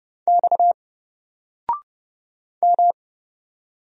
Morse code, heard as a keyed tone, about 30 wpm, 700 Hz. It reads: X M
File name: Single-Letter-Number-30wpm